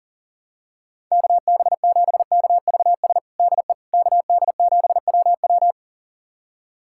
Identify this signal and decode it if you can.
Morse code, heard as a keyed tone, about 40 wpm, 700 Hz. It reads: KB7KVS DE KD7WW
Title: USA Callsign DE Callsign 40wpm